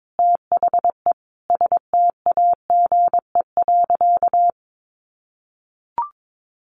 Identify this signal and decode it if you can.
Morse code, heard as a keyed tone, about 22 wpm, 700 Hz. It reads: THE STAGE.
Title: Sentences from Top 1200 Words 22wpm